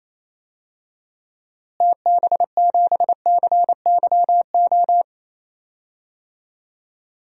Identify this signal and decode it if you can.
Morse code, heard as a keyed tone, about 28 wpm, 700 Hz. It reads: TB7CYO
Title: DXCC-100 Call Signs - 6 Characters - Repeated 3x - 28wpm